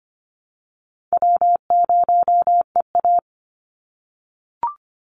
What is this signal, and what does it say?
Morse code, heard as a keyed tone, about 25 wpm, 700 Hz. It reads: W0EA
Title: Easy Call Signs - 25wpm